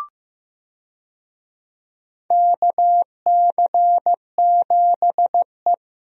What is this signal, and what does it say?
Morse code, heard as a keyed tone, about 15 wpm, 700 Hz. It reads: KC7E
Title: USA Call Signs - 4 Characters - 15wpm